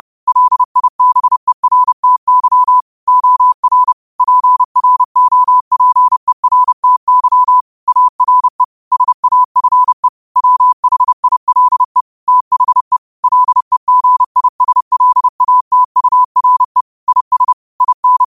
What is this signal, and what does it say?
Morse code, heard as a keyed tone, about 30 wpm, 1000 Hz. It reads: LIBERTY OR PROPERTY ARE SAFE WHILE THE LEGISLATURE IS IN